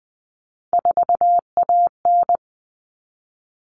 Morse code, heard as a keyed tone, about 20 wpm, 700 Hz. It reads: 4AN
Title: All Cut Numbers - 3-Digit 20wpm